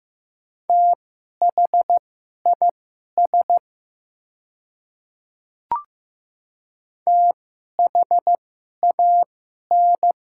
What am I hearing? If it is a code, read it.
Morse code, heard as a keyed tone, about 15 wpm, spaced as for 11 wpm, 700 Hz. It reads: THIS THAN
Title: 2x Character Spacing - Top 100 Words - 15wpm